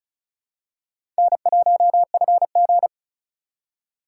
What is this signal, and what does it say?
Morse code, heard as a keyed tone, about 34 wpm, 700 Hz. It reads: N1FG
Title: USA Call Signs - 4 Characters - 3x Character Spacing - 35wpm